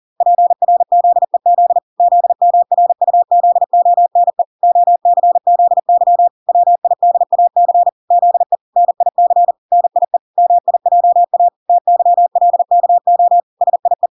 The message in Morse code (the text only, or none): PRZEZ ZMRUZONE OCZY WIDAC ZE NIC NIE MIJA TYLKO SIE